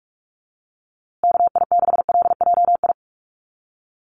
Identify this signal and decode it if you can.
Morse code, heard as a keyed tone, about 44 wpm, 700 Hz. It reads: KI6LJI
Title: USA Call Signs Repeated 2x 45wpm